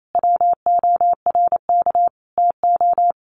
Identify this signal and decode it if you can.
Morse code, heard as a keyed tone, about 28 wpm, 700 Hz. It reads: WORK TO
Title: Sets of 5 Words Encore2 28wpm